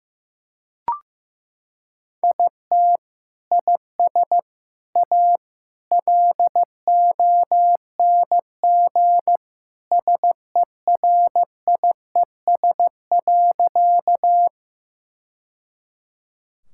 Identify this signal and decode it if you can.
Morse code, heard as a keyed tone, about 15 wpm, 700 Hz. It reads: IT IS A LONG SERIES.
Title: Sentences from Top 1000 Words - Review Part 5 - 15wpm